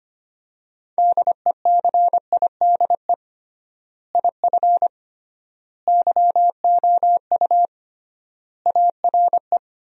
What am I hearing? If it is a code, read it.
Morse code, heard as a keyed tone, about 25 wpm, 700 Hz. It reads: DECIDE IF YOU ARE
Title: 3x Word-Spacing - Sentences from Top 500 Words - 25wpm